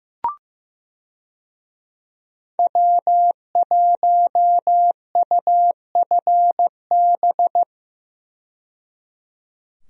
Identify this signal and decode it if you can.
Morse code, heard as a keyed tone, about 15 wpm, 700 Hz. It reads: W1UFB
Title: USA Call Signs - 5 Characters - 15wpm